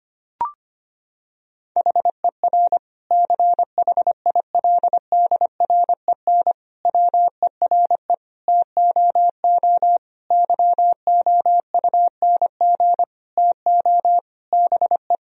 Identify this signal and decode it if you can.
Morse code, heard as a keyed tone, about 25 wpm, 700 Hz. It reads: HER CHILDREN WERE TOO YOUNG TO BE
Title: Sentences from Top 1000 Words - Review Part 4 - 25wpm